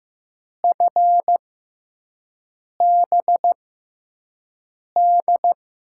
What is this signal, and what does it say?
Morse code, heard as a keyed tone, about 15 wpm, spaced as for 5 wpm, 700 Hz. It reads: FBD
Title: USA Call Signs - 5 Characters - 6x Character Spacing - 15wpm